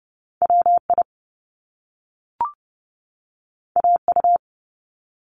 Morse code, heard as a keyed tone, about 30 wpm, 700 Hz. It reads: WI AU
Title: Top Two Letter Combinations in Words 30wpm